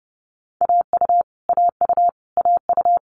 Morse code, heard as a keyed tone, about 30 wpm, 700 Hz. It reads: AU AU AU
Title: Multi-letter Phonemes Repeated 3x 30wpm